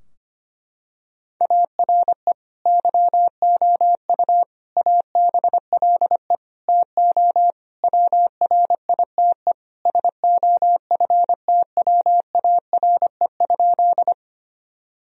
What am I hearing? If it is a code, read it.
Morse code, heard as a keyed tone, about 25 wpm, 700 Hz. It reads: ARE YOU ABLE TO WRITE SOFTWARE?
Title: Sentences from Top 1000 Words - Review Part 3 - 25wpm